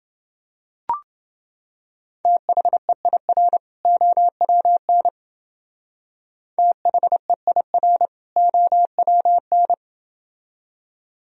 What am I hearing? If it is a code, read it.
Morse code, heard as a keyed tone, about 28 wpm, 700 Hz. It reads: THEIR OWN THEIR OWN